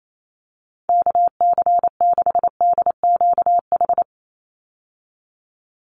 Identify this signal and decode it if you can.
Morse code, heard as a keyed tone, about 28 wpm, 700 Hz. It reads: KC6DQH
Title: USA Call Signs 28wpm